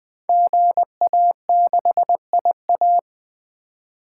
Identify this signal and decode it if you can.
Morse code, heard as a keyed tone, about 20 wpm, 700 Hz. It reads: GA6IA
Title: Code Groups - 5 Letter:Numbers - 20wpm